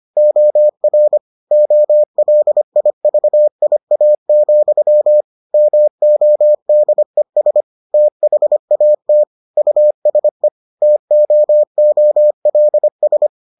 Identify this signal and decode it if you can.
Morse code, heard as a keyed tone, about 25 wpm, 600 Hz. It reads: OR OLIVIA, MODES THAT USE TOOLS